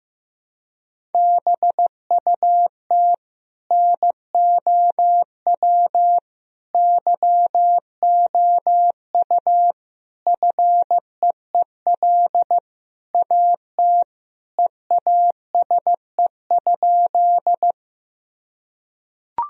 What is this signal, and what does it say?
Morse code, heard as a keyed tone, about 15 wpm, 700 Hz. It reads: BUT NOW YOU FEEL AT EASE?
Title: Sentences from Top 1000 Words - Review Part 1 - 15wpm